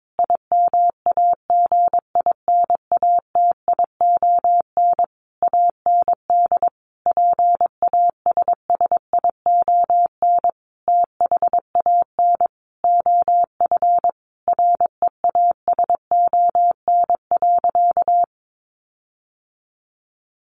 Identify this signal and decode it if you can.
Morse code, heard as a keyed tone, about 22 wpm, 700 Hz. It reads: IMAGINATION AND PASSION THAN OF REASON.
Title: Sentences from Top 1300 Words 22wpm